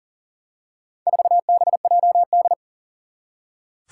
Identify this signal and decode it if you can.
Morse code, heard as a keyed tone, about 42 wpm, 700 Hz. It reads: 4BJD
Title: Four Challenging Letter-Number Combinations 40wpm